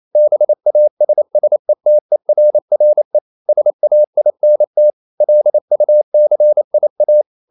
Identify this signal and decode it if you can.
Morse code, heard as a keyed tone, about 28 wpm, 600 Hz. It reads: BASSETERRE SAINT LUCIA
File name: CW / Practice / NationalCapitals28